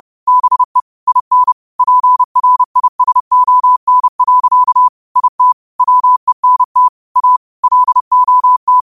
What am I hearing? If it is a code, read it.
Morse code, heard as a keyed tone, about 30 wpm, 1000 Hz. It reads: BE IN PRISON. IT WENT A LOT
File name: qotdfiles / 30wpm